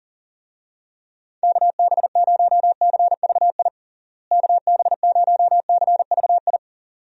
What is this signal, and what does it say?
Morse code, heard as a keyed tone, about 40 wpm, 700 Hz. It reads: KB0CVI KB0CVI
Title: USA Call Signs - 6 Characters - Repeated 2x - 40wpm